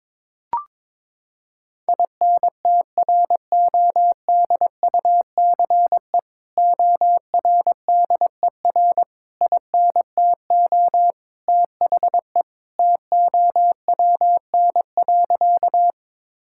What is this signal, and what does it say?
Morse code, heard as a keyed tone, about 22 wpm, 700 Hz. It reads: INTRODUCE ORDER INTO THE TOWN.
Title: Sentences from Top 1600 Words 22wpm